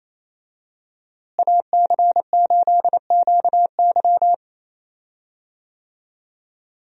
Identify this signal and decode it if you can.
Morse code, heard as a keyed tone, about 28 wpm, 700 Hz. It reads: AC8QY
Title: USA Call Signs - 5 Characters - Repeated 2x - 28wpm